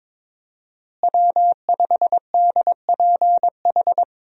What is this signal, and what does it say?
Morse code, heard as a keyed tone, about 21 wpm, 700 Hz. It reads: W5DPH
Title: USA Call Signs - 5 Characters - Repeated 2x - 22wpm